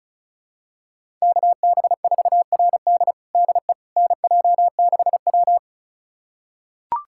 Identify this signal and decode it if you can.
Morse code, heard as a keyed tone, about 36 wpm, 700 Hz. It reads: KB4RD DE NJ6W
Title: USA Callsign DE Callsign 35wpm